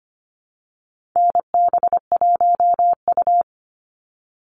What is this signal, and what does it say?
Morse code, heard as a keyed tone, about 25 wpm, 700 Hz. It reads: NB1U
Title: Call Signs Repeated 3x - 25wpm